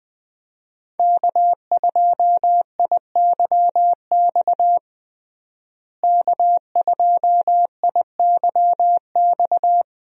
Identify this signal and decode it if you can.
Morse code, heard as a keyed tone, about 20 wpm, 700 Hz. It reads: K2IYX K2IYX
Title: USA Call Signs - 5 Characters - Repeated 2x - 20wpm